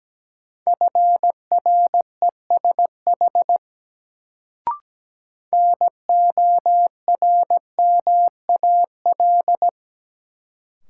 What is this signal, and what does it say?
Morse code, heard as a keyed tone, about 17 wpm, 700 Hz. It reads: FRESH NORMAL
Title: Sentences from Top 1200 Words - New Word Drill Alt - 17wpm